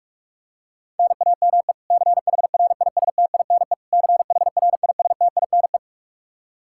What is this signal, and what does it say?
Morse code, heard as a keyed tone, about 41 wpm, 700 Hz. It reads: NAME CHRISTINE CHRISTINE
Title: Practice QSO - Exchange 1 - Beginning - 45wpm